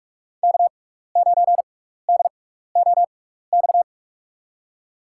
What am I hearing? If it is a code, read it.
Morse code, heard as a keyed tone, about 46 wpm, spaced as for 15 wpm, 690 Hz. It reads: K9DOX
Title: USA Call Signs - 5 Characters - 6x Character Spacing - 45wpm